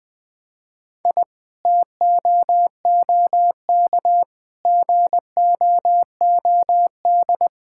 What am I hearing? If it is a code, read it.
Morse code, heard as a keyed tone, about 20 wpm, 700 Hz. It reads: I TOOK GOOD
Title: Sentences from Top 1000 Words - Review Part 6 - 20wpm